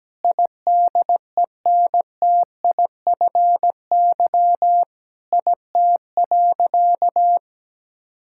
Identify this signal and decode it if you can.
Morse code, heard as a keyed tone, about 17 wpm, 700 Hz. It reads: IDENTIFY IT.
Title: Sentences from Top 1000 Words - Review Part 6 - 17wpm